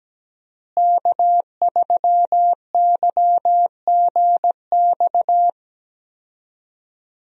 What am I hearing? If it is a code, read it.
Morse code, heard as a keyed tone, about 17 wpm, 700 Hz. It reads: K3YGX
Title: USA Call Signs - 5 Characters - Repeated 2x - 17wpm